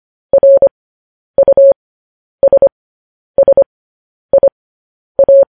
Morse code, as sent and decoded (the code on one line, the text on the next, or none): .-. ..- ... ... .. .-
RUSSIA